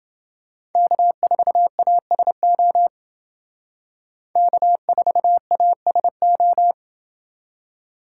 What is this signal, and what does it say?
Morse code, heard as a keyed tone, about 29 wpm, 700 Hz. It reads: K4ASO K4ASO